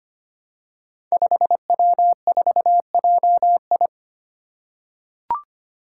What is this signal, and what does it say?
Morse code, heard as a keyed tone, about 25 wpm, 700 Hz. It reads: 5W4JI
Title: Five Challenging Letter-Number Combinations 25wpm